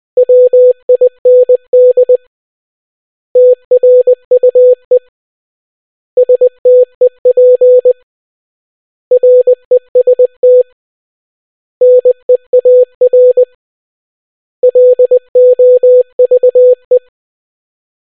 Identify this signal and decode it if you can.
Morse code, heard as a keyed tone, about 20 wpm, 500 Hz. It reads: WIND TRUE STEP REST NEAR LOVE